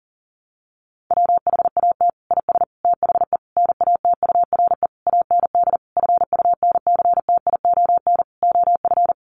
Morse code, heard as a keyed tone, about 40 wpm, 700 Hz. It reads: WHAT IS THE NATURE AND FUNCTION OF
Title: Sentences from Top 1100 Words 40wpm